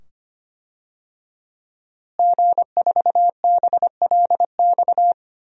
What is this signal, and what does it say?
Morse code, heard as a keyed tone, about 25 wpm, 700 Hz. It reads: G4BLX
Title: DXCC-100 Call Signs - 5 Characters - Repeated 2x - 25wpm